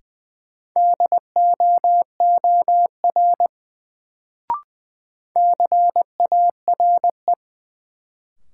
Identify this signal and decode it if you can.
Morse code, heard as a keyed tone, about 20 wpm, 700 Hz. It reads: DOOR CARE
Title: Sentences from Top 300 Words - New Word Drill - 20wpm